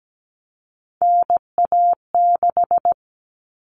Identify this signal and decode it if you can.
Morse code, heard as a keyed tone, about 17 wpm, 700 Hz. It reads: NA6
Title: Common Cut Numbers - 3-Digit 17wpm